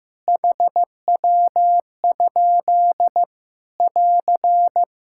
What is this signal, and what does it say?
Morse code, heard as a keyed tone, about 15 wpm, 700 Hz. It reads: HW? <AR>
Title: K1USN SST QSO Practice 15wpm